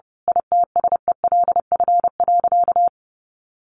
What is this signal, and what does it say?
Morse code, heard as a keyed tone, about 30 wpm, 700 Hz. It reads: ITSELF.
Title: A Sound Of Thunder 30wpm